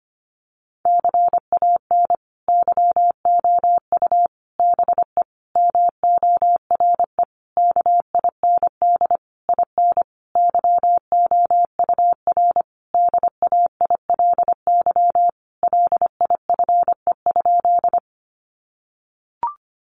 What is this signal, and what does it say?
Morse code, heard as a keyed tone, about 25 wpm, 700 Hz. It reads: CAN YOU BE MORE KIND IN YOUR DAILY LIFE?
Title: Sentences from Top 1100 Words 25wpm